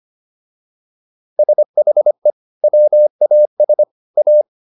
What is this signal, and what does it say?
Morse code, heard as a keyed tone, about 25 wpm, 600 Hz. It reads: SHE WAS A